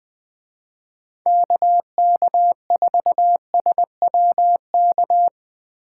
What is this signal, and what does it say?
Morse code, heard as a keyed tone, about 20 wpm, 700 Hz. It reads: KK4SWK
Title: Speed Racing - USA 2x3 Call Signs 17wpm